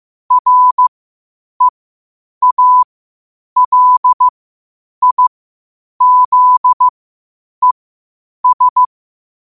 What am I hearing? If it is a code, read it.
Morse code, heard as a keyed tone, about 15 wpm, spaced as for 8 wpm, 1000 Hz. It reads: REALIZES
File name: qotdfiles / 8wpm